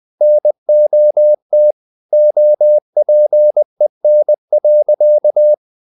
Morse code, heard as a keyed tone, about 20 wpm, 600 Hz. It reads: NOT OPEN.
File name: the old man and the sea / the old man and the sea part 1 20wpm